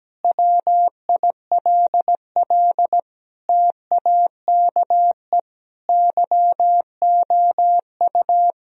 Morse code, heard as a keyed tone, about 17 wpm, 700 Hz. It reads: WILL TAKE YOU
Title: Sentences from Top 1000 Words - Review Part 3 - 17wpm